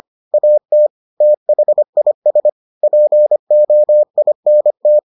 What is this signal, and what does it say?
Morse code, heard as a keyed tone, about 25 wpm, 600 Hz. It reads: AT THIS POINT